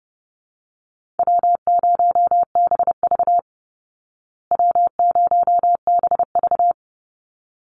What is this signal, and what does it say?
Morse code, heard as a keyed tone, about 30 wpm, 700 Hz. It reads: W0BV W0BV
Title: Easy Call Signs Repeated 2x - 30wpm